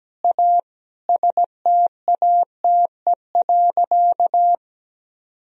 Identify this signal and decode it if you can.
Morse code, heard as a keyed tone, about 17 wpm, 700 Hz. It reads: A STATE.
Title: Sentences from Top 1000 Words - Review Part 2 - 17wpm